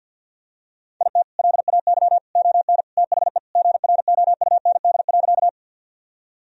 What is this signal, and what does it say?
Morse code, heard as a keyed tone, about 50 wpm, 700 Hz. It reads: IT LAY ON THE GROUND.
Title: Sentences from Top 1000 Words - Review Part 2 - 50wpm